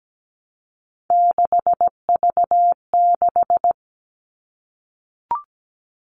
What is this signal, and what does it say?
Morse code, heard as a keyed tone, about 17 wpm, 700 Hz. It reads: 6V6
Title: All Cut Numbers - 3-Digit 17wpm